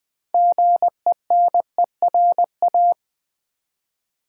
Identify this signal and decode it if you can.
Morse code, heard as a keyed tone, about 20 wpm, 700 Hz. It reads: GENERA
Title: Build A Word - 3 Letters At A Time - 9-Letter Words - 20wpm